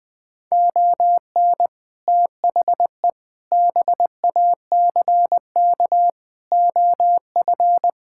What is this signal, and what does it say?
Morse code, heard as a keyed tone, about 21 wpm, 700 Hz. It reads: ON THE BACK OF
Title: Build A Phrase - 4-Word Phrases - 20wpm